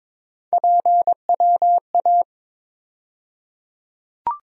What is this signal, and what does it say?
Morse code, heard as a keyed tone, about 22 wpm, 700 Hz. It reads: PWA
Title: USA Call Sign Postfixes - 3 Letters 22wpm